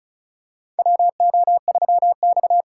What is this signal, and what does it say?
Morse code, heard as a keyed tone, about 34 wpm, 700 Hz. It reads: WO3X
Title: USA Call Signs - 4 Characters - 4x Character Spacing - 35wpm